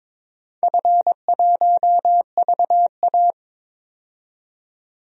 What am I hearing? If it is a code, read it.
Morse code, heard as a keyed tone, about 22 wpm, 700 Hz. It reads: F1VA
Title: DXCC-100 Call Signs - 4 Characters - 22wpm